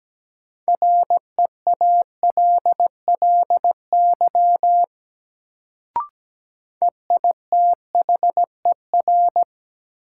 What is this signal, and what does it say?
Morse code, heard as a keyed tone, about 17 wpm, 700 Hz. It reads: REALLY EITHER
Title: Sentences from Top 600 Words - New Word Drill - 17wpm